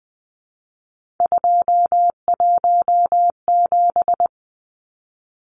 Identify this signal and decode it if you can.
Morse code, heard as a keyed tone, about 20 wpm, 700 Hz. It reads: 217
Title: Numbers - 3-Digit 20wpm